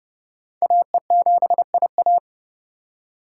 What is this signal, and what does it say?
Morse code, heard as a keyed tone, about 28 wpm, 700 Hz. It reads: AE7IA